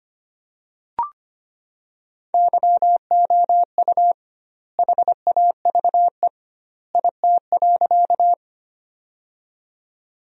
Morse code, heard as a keyed tone, about 25 wpm, 700 Hz. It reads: YOU HAVE IT.
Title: 2x Word-Spacing - Sentences from Top 100 Words - 25wpm